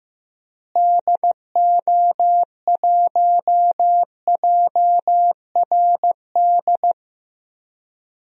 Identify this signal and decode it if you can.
Morse code, heard as a keyed tone, about 15 wpm, 700 Hz. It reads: DO1JRD
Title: DXCC-100 Call Signs - 6 Characters - Repeated 3x - 15wpm